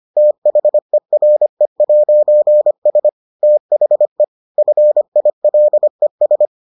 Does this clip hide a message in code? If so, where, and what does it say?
Yes: Morse code from 0.2 to 6.5 s, heard as a keyed tone, about 25 wpm, 600 Hz. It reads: THERE'S THE FILES